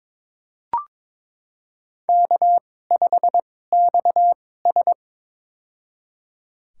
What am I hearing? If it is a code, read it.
Morse code, heard as a keyed tone, about 22 wpm, spaced as for 16 wpm, 700 Hz. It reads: K5XS
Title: USA Call Signs - 4 Characters - 2x Character Spacing - 22wpm